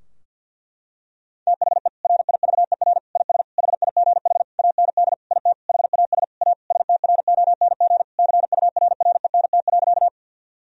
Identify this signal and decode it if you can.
Morse code, heard as a keyed tone, about 50 wpm, 700 Hz. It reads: THE RIVER IS HIGH AND IT HAS A STRONG CURRENT.
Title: Sentences from Top 1000 Words - Review Part 4 - 50wpm